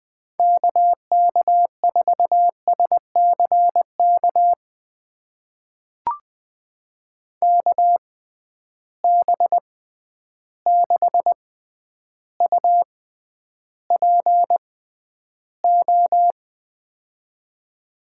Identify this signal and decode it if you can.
Morse code, heard as a keyed tone, about 20 wpm, spaced as for 10 wpm, 700 Hz. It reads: KK4SCK KB6UPO
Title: USA Call Signs - 6 Characters - 6x Character Spacing - 20wpm